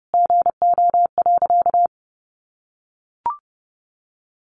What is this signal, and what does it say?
Morse code, heard as a keyed tone, about 30 wpm, 700 Hz. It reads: GO.
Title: Additional Easy Phrases 30wpm